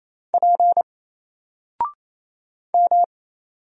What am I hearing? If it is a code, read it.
Morse code, heard as a keyed tone, about 27 wpm, 700 Hz. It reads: P M
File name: Single-Letter-Number-28wpm